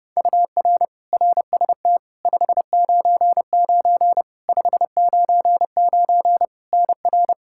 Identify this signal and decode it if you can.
Morse code, heard as a keyed tone, about 32 wpm, 700 Hz. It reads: UR RST 599 599 NR
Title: Practice QSO - Exchange 1 - Beginning - 30wpm